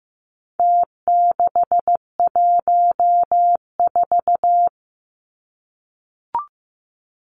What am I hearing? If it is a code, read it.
Morse code, heard as a keyed tone, about 15 wpm, 700 Hz. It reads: T614
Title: Numbers - Contest Style - 4-Digit 15wpm